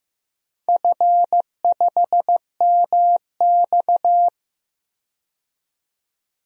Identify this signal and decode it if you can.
Morse code, heard as a keyed tone, about 15 wpm, 700 Hz. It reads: F5MX
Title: DXCC-100 Call Signs - 4 Characters - 15wpm